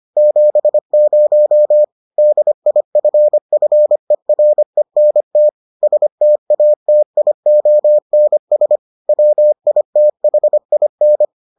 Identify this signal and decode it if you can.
Morse code, heard as a keyed tone, about 25 wpm, 600 Hz. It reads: 70 DIFFERENT STATIONS WITHIN